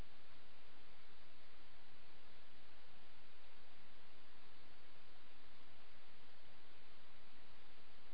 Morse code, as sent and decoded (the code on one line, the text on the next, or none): none